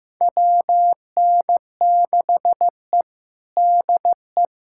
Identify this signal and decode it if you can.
Morse code, heard as a keyed tone, about 15 wpm, 700 Hz. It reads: WN6E DE
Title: USA Callsign DE Callsign - 4 Characters - 15wpm